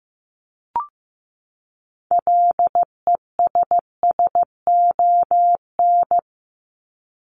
Morse code, heard as a keyed tone, about 15 wpm, 700 Hz. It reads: LESSON
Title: Top 1001-1250 Words 15wpm